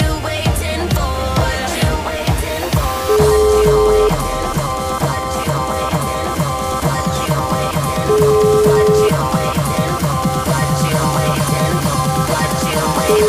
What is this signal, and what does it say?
Ringback tone (Europe, 425 Hz), with noise.